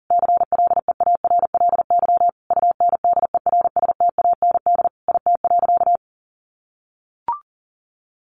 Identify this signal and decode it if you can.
Morse code, heard as a keyed tone, about 40 wpm, 700 Hz. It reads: CLEARLY UNDERSTAND IT.
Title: Sentences from Top 1400 Words 40wpm